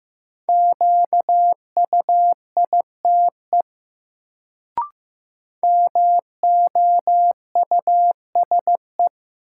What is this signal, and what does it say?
Morse code, heard as a keyed tone, about 15 wpm, 700 Hz. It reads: QUITE MOUSE
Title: Sentences from Top 700 Words - New Word Drill - 15wpm